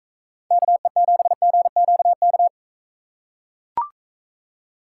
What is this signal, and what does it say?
Morse code, heard as a keyed tone, about 44 wpm, 700 Hz. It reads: KE7GQK